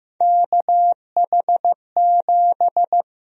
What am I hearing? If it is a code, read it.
Morse code, heard as a keyed tone, about 15 wpm, 700 Hz. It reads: KH7
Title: USA Call Sign Prefixes - Repeated 2x - 15wpm